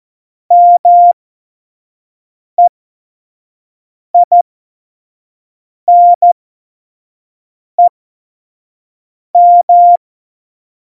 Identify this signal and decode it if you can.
Morse code, heard as a keyed tone, about 14 wpm, spaced as for 5 wpm, 700 Hz. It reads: MEINEM
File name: Morsekurs / Witze / Witz 25. Juni 2019, 25 F70